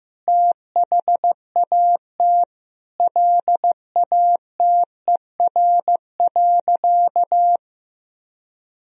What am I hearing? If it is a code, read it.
Morse code, heard as a keyed tone, about 15 wpm, 700 Hz. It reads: THAT LATER.
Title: Sentences from Top 1000 Words - Review Part 4 - 15wpm